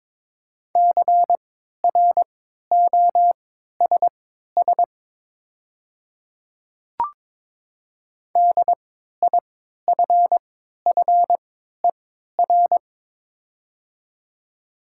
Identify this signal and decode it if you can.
Morse code, heard as a keyed tone, about 22 wpm, spaced as for 12 wpm, 700 Hz. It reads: CROSS DIFFER
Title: 3x Character Spacing - Top 500 Words - 22wpm